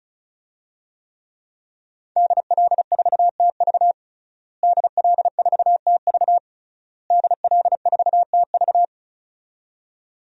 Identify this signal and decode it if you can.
Morse code, heard as a keyed tone, about 36 wpm, 700 Hz. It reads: DL4TV DL4TV DL4TV
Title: DXCC-100 Call Signs - 5 Characters - Repeated 3x - 35wpm